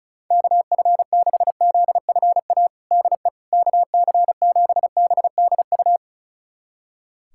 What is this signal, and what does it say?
Morse code, heard as a keyed tone, about 36 wpm, 700 Hz. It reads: KF6ZFA DE KC7BDU
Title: USA Callsign DE Callsign - 6 Characters - 35wpm